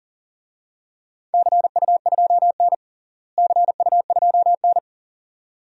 Morse code, heard as a keyed tone, about 40 wpm, 700 Hz. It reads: CU2N CU2N
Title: DXCC-100 Call Signs - 4 Characters - Repeated 2x - 40wpm